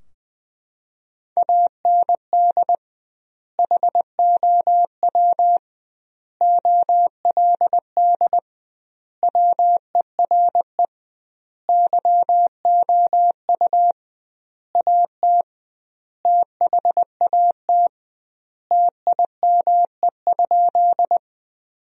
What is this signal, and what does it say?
Morse code, heard as a keyed tone, about 20 wpm, 700 Hz. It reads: AND HOW OLD WERE YOU AT THAT TIME?
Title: 2x Word-Spacing - Sentences from Top 200 Words - 20wpm